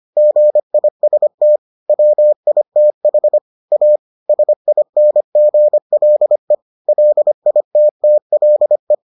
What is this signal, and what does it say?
Morse code, heard as a keyed tone, about 25 wpm, 600 Hz. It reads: GIST WITH A SINGLE LITTLE